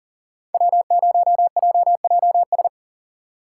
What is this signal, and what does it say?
Morse code, heard as a keyed tone, about 52 wpm, 700 Hz. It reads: W0JJS